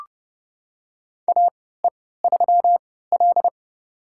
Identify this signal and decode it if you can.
Morse code, heard as a keyed tone, about 31 wpm, spaced as for 17 wpm, 700 Hz. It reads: AE3L
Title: USA Call Signs - 4 Characters - 3x Character Spacing - 30wpm